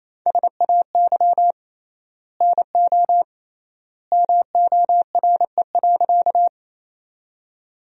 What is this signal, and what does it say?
Morse code, heard as a keyed tone, about 28 wpm, 700 Hz. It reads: SAY NO MORE.
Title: 3x Word-Spacing - Sentences from Top 200 Words - 28wpm